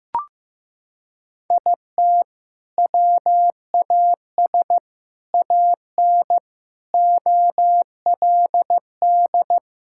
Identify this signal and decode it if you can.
Morse code, heard as a keyed tone, about 15 wpm, 700 Hz. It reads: IT WAS AN OLD
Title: Sentences from Top 1000 Words - Review Part 5 - 15wpm